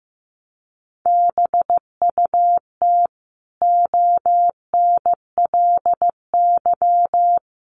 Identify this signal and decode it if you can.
Morse code, heard as a keyed tone, about 14 wpm, 700 Hz. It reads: BUT ONLY
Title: Sentences from Top 1100 Words 15wpm